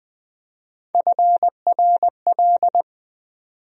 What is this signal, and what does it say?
Morse code, heard as a keyed tone, about 20 wpm, 700 Hz. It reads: FRL
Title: USA Call Sign Postfixes - 3 Letters 20wpm